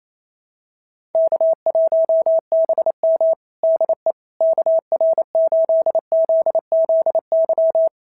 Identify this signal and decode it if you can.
Morse code, heard as a keyed tone, about 28 wpm, 690 Hz. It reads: K1BM DE KR8ZZY
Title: SOTA QSO Practice 28wpm